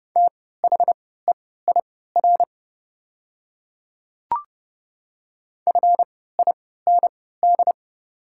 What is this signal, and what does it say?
Morse code, heard as a keyed tone, about 29 wpm, spaced as for 17 wpm, 700 Hz. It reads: THEIR FIND
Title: 3x Character Spacing - Top 100 Words - 30wpm